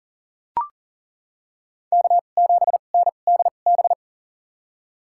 Morse code, heard as a keyed tone, about 40 wpm, spaced as for 29 wpm, 700 Hz. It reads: K7NDB
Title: USA Call Signs - 5 Characters - 2x Character Spacing - 40wpm